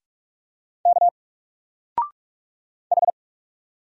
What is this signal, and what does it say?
Morse code, heard as a keyed tone, about 44 wpm, 700 Hz. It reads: K H
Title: Single Character (Letter:Number:Punctuation:Prosign) 45wpm